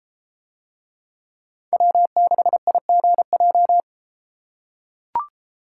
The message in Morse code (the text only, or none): W6IGJ